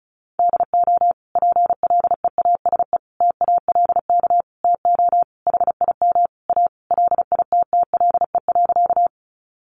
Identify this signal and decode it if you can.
Morse code, heard as a keyed tone, about 34 wpm, 700 Hz. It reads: DO PLEASE TALK TO HIM A LITTLE.
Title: Sentences from Top 1500 Words 35wpm